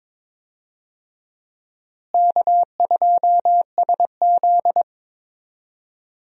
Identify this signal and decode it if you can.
Morse code, heard as a keyed tone, about 22 wpm, 700 Hz. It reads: K2SZ
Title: USA Call Signs - 4 Characters - Repeated 2x - 22wpm